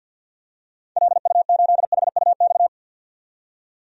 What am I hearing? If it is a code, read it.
Morse code, heard as a keyed tone, about 51 wpm, 700 Hz. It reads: LU8HUX
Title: DXCC-100 Call Signs - 6 Characters - 50wpm